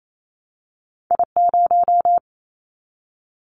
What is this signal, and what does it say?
Morse code, heard as a keyed tone, about 28 wpm, 700 Hz. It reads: I0
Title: Callsign Prefixes for Top 25 DXCC - 28wpm